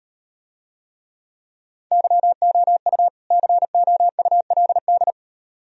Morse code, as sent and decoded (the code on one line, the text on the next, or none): -.-- --- ..- / -.-. --- ..- .-.. -..
YOU COULD